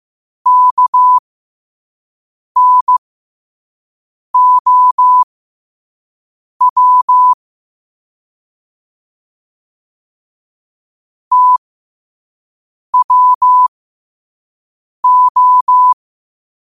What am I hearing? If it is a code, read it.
Morse code, heard as a keyed tone, about 15 wpm, spaced as for 5 wpm, 1000 Hz. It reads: KNOW TWO
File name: qotdfiles / 5wpm